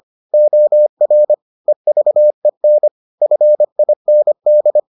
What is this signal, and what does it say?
Morse code, heard as a keyed tone, about 25 wpm, 600 Hz. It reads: OR EVEN FIND